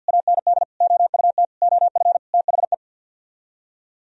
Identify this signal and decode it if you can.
Morse code, heard as a keyed tone, about 64 wpm, 700 Hz. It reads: AND OUT OF THE